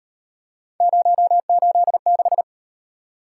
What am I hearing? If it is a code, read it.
Morse code, heard as a keyed tone, about 38 wpm, 700 Hz. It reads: 086